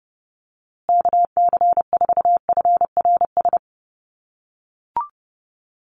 Morse code, heard as a keyed tone, about 30 wpm, 700 Hz. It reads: KC4FRS
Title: USA Call Signs Repeated 3x 30wpm